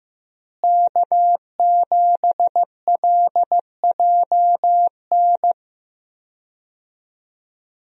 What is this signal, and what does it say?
Morse code, heard as a keyed tone, about 15 wpm, 700 Hz. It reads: K7LJN
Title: USA Call Signs - 5 Characters - Repeated 2x - 15wpm